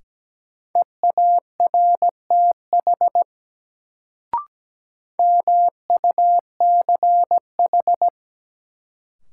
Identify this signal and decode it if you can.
Morse code, heard as a keyed tone, about 17 wpm, 700 Hz. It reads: EARTH MUCH
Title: Sentences from Top 200 Words - New Word Drill - 17wpm